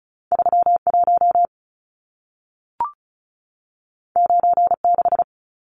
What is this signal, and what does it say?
Morse code, heard as a keyed tone, about 36 wpm, 700 Hz. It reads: 31 96
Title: Numbers - 2-Digit 35wpm